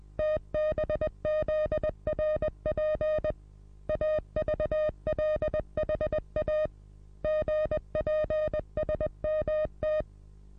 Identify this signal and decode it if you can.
Morse code, heard as a keyed tone, about 20 wpm, 620 Hz. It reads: TBZRP AVLHA GPSMT
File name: repos / Morsekurs / DJ4UF Morsen27